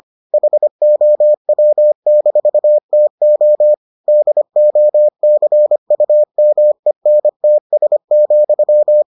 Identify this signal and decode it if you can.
Morse code, heard as a keyed tone, about 25 wpm, 600 Hz. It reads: HOW-TO DOCUMENTS,